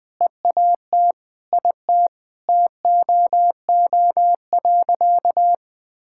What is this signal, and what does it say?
Morse code, heard as a keyed tone, about 20 wpm, 700 Hz. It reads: EAT IT TOO.
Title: Sentences with 8-Word Phrases - Spoken Beginning - 20wpm